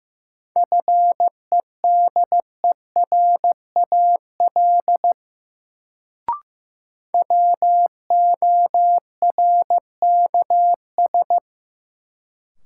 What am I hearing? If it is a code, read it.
Morse code, heard as a keyed tone, about 15 wpm, 700 Hz. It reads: FEDERAL WORKS
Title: Sentences from Top 1100 Words - New Word Drill Alt - 15wpm